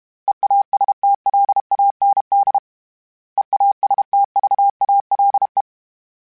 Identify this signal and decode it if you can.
Morse code, heard as a keyed tone, about 32 wpm, 800 Hz. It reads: EASTLAND EASTVALE
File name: fromSMs / FromSMs mp3s / Cities@30